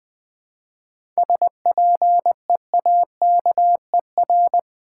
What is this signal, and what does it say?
Morse code, heard as a keyed tone, about 20 wpm, 700 Hz. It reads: SPEAKER
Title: Seven Letter Words - 20wpm